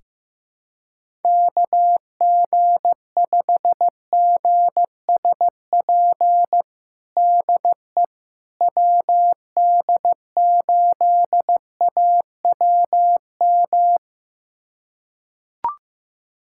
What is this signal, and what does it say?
Morse code, heard as a keyed tone, about 15 wpm, 700 Hz. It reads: KG5GSP DE WD8AWM
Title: USA Callsign DE Callsign 15wpm